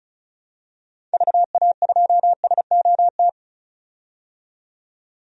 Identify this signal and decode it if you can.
Morse code, heard as a keyed tone, about 36 wpm, 690 Hz. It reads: VA2SOT
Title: DXCC-100 Call Signs - 6 Characters - 35wpm